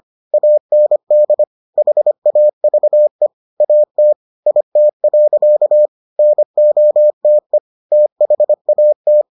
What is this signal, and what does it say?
Morse code, heard as a keyed tone, about 25 wpm, 600 Hz. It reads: AND HAVE AT IT. NOTE THAT